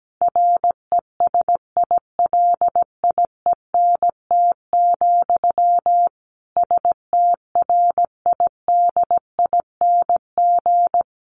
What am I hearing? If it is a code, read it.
Morse code, heard as a keyed tone, about 17 wpm, 700 Hz. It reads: RESILIENT, STRIDING
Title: A Sound Of Thunder 17wpm